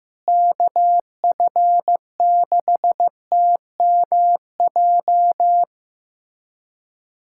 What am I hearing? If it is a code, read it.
Morse code, heard as a keyed tone, about 15 wpm, 700 Hz. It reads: KF6TMJ
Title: USA Callsign DE Callsign - 6 Characters - 15wpm